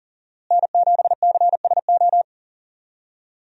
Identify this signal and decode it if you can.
Morse code, heard as a keyed tone, about 40 wpm, 700 Hz. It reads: N7CSO
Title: USA Call Signs - 5 Characters - Repeated 3x - 40wpm